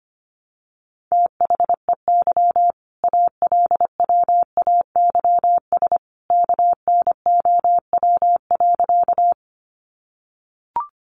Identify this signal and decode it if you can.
Morse code, heard as a keyed tone, about 25 wpm, 700 Hz. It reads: THEY ALWAYS KNOW.
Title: Sentences from Top 300 Words 25wpm